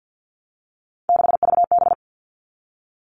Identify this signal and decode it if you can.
Morse code, heard as a keyed tone, about 50 wpm, 700 Hz. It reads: 6VB
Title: All Cut Numbers - 3-Digit 50wpm